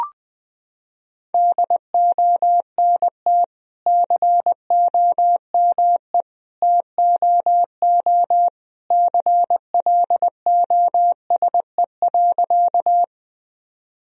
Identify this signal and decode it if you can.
Morse code, heard as a keyed tone, about 21 wpm, 700 Hz. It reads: DONT COME TOO CLOSE.
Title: Sentences from Top 1000 Words - Review Part 1 - 20wpm